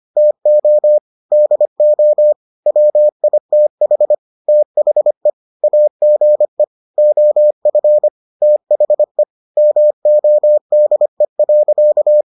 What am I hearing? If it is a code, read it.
Morse code, heard as a keyed tone, about 25 wpm, 600 Hz. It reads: TO DO WITH THE AGE OF THE MODE.